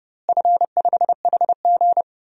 Morse code, heard as a keyed tone, about 30 wpm, 700 Hz. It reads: F5HG
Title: DXCC-100 Call Signs - 4 Characters - Repeated 3x - 30wpm